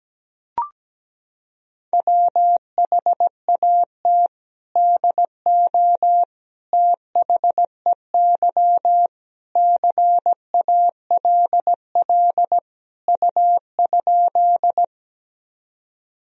Morse code, heard as a keyed tone, about 17 wpm, 700 Hz. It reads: WHAT DO THEY CALL U?
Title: Sentences from Top 100 Words and Abbreviations 17wpm